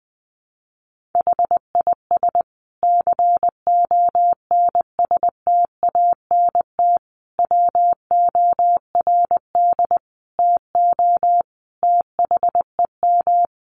Morse code, heard as a keyed tone, about 20 wpm, 700 Hz. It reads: HIS CONSTANT WORD TO THEM
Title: Sentences from Top 1500 Words 20wpm